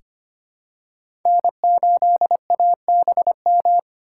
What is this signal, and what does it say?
Morse code, heard as a keyed tone, about 25 wpm, 700 Hz. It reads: N8ABM
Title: USA Call Signs - 2x Character Spacing - 25wpm